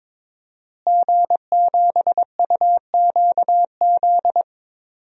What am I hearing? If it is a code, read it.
Morse code, heard as a keyed tone, about 22 wpm, 700 Hz. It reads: G7UQZ
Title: DXCC-100 Call Signs - 5 Characters - Repeated 2x - 22wpm